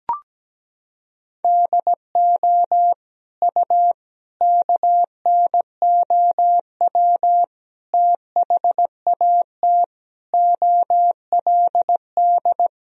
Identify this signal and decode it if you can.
Morse code, heard as a keyed tone, about 17 wpm, 700 Hz. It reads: DO U KNOW THAT OLD
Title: Sentences from Top 400 Words and Abbreviations 17wpm